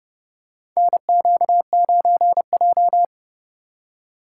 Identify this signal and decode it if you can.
Morse code, heard as a keyed tone, about 29 wpm, 700 Hz. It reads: NQ9J
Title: USA Call Signs - 2x Character Spacing - 30wpm